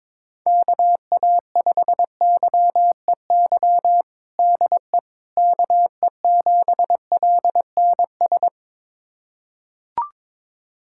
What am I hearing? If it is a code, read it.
Morse code, heard as a keyed tone, about 22 wpm, 700 Hz. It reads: KA5YEY DE KE7LNS
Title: USA Callsign DE Callsign 22wpm